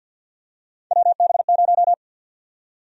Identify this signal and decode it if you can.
Morse code, heard as a keyed tone, about 50 wpm, 700 Hz. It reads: WB0
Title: USA Call Sign Prefixes - 50wpm